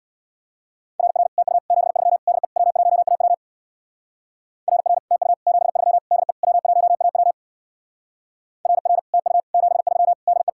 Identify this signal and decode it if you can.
Morse code, heard as a keyed tone, about 61 wpm, 700 Hz. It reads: RR TU 73 DE W1AW RR TU 73 DE W1AW RR TU 73 DE
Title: Speed-Racing Common QSO Phrases Encore - 50wpm